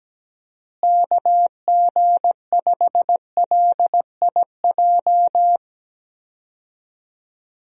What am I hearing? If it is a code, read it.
Morse code, heard as a keyed tone, about 17 wpm, 700 Hz. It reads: KG5LIJ
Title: USA Call Signs - 6 Characters - Repeated 3x - 17wpm